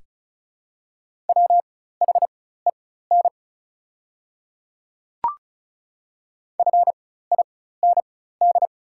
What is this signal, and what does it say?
Morse code, heard as a keyed tone, about 36 wpm, spaced as for 16 wpm, 690 Hz. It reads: WHEN FIND
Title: 4x Character Spacing - Top 100 Words - 35wpm